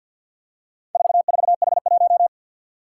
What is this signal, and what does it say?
Morse code, heard as a keyed tone, about 51 wpm, 700 Hz. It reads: 44H1
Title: Four Challenging Letter-Number Combinations 50wpm